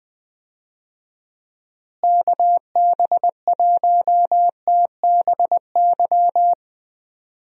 Morse code, heard as a keyed tone, about 20 wpm, 700 Hz. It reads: KB1TBY
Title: USA Call Signs - 6 Characters - Repeated 3x - 20wpm